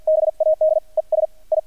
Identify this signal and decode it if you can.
Morse code, heard as a keyed tone, about 48 wpm, 630 Hz. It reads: BAD ES I
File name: archived / CRSnet